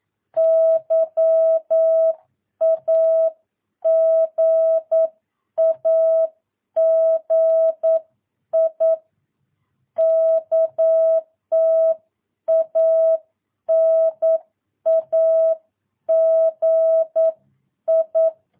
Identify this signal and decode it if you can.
Morse code, heard as a keyed tone, about 9 wpm, 650 Hz. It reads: YAGAGI KTANAGI
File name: jh8cbh / 2022cw